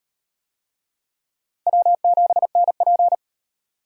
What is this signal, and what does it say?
Morse code, heard as a keyed tone, about 34 wpm, 700 Hz. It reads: W7NP